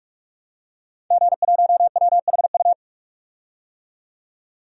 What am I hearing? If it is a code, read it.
Morse code, heard as a keyed tone, about 47 wpm, 700 Hz. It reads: G1WHU
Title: DXCC-100 Call Signs - 5 Characters - 45wpm